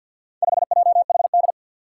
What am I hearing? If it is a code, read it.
Morse code, heard as a keyed tone, about 51 wpm, 700 Hz. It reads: 5JHD
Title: Four Challenging Letter-Number Combinations 50wpm